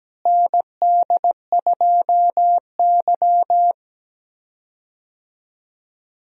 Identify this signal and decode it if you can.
Morse code, heard as a keyed tone, about 17 wpm, 700 Hz. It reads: ND2Y
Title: USA Call Signs - 4 Characters - Repeated 2x - 17wpm